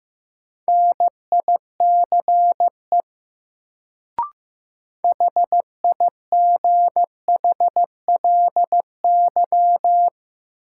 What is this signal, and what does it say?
Morse code, heard as a keyed tone, about 15 wpm, 700 Hz. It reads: NICE HIGHLY
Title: Sentences from Top 1100 Words - New Word Drill Alt - 15wpm